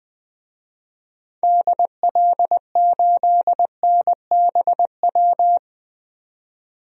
Morse code, heard as a keyed tone, about 20 wpm, 700 Hz. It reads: DL8NBW
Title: DXCC-100 Call Signs - 6 Characters - 20wpm